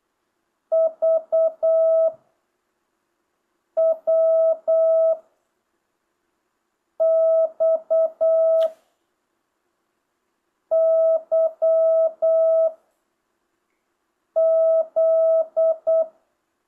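Morse code, heard as a keyed tone, about 8 wpm, 660 Hz. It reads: V W X Y Z
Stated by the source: Morse lesson from JH8CBH's Zoom Morse course, session 5: letters V–Z, random 5-character and 10-character groups with 4-second gaps.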